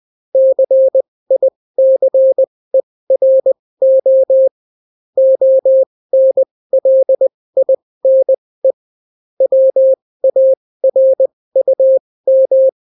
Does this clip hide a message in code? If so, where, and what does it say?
Yes: Morse code from 0.3 to 12.7 s, heard as a keyed tone, about 20 wpm, spaced as for 16 wpm, 530 Hz. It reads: CICERO ONLINE WARUM